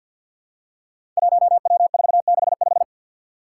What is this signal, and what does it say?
Morse code, heard as a keyed tone, about 52 wpm, 700 Hz. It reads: 1W465
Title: Five Challenging Letter-Number Combinations 50wpm